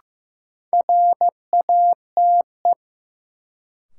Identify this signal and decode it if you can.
Morse code, heard as a keyed tone, about 15 wpm, 700 Hz. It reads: RATE
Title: Sentences from Top 700 Words - New Word Drill - 15wpm